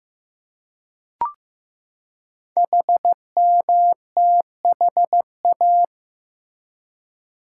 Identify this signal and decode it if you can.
Morse code, heard as a keyed tone, about 15 wpm, 700 Hz. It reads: HMTHA
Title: Code Groups - 5 Letters - 15wpm